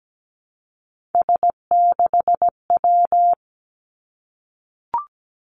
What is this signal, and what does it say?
Morse code, heard as a keyed tone, about 17 wpm, 700 Hz. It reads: S6W
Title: Three Challenging Letter-Number Combinations 17wpm